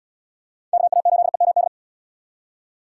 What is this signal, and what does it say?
Morse code, heard as a keyed tone, about 72 wpm, 700 Hz. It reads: KA8EMD